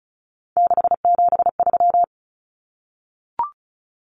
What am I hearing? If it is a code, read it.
Morse code, heard as a keyed tone, about 34 wpm, 700 Hz. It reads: 673
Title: Common Cut Numbers - 3-Digit 35wpm